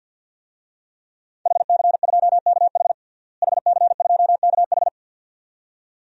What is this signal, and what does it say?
Morse code, heard as a keyed tone, about 51 wpm, 700 Hz. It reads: HC2KH HC2KH
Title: DXCC-100 Call Signs - 5 Characters - Repeated 2x - 50wpm